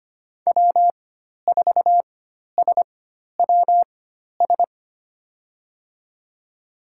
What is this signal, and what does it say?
Morse code, heard as a keyed tone, about 25 wpm, spaced as for 12 wpm, 700 Hz. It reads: W4SWS
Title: USA Call Signs - 5 Characters - 4x Character Spacing - 25wpm